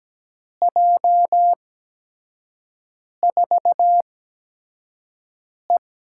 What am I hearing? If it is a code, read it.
Morse code, heard as a keyed tone, about 17 wpm, spaced as for 5 wpm, 700 Hz. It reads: J4E
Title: USA Call Signs - 8x Character Spacing - 17wpm